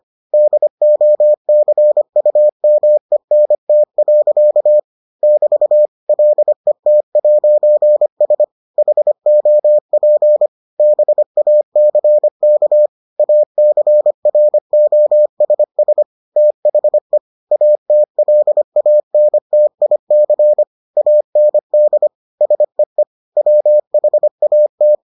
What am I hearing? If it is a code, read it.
Morse code, heard as a keyed tone, about 25 wpm, 600 Hz. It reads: DOCUMENT. <BT> LET'S HOP BACK ACROSS THE ATLANTIC AND SEE WHAT